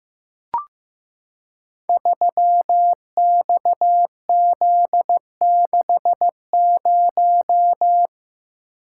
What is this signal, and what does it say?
Morse code, heard as a keyed tone, about 15 wpm, 700 Hz. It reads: 3XZ60
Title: Code Groups - 5 Letter:Numbers - 15wpm